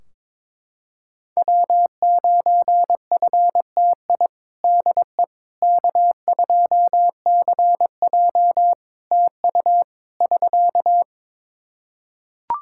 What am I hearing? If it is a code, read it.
Morse code, heard as a keyed tone, about 22 wpm, 700 Hz. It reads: W9FTI DE K2CJ TU <SK>
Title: K1USN SST QSO Practice 22wpm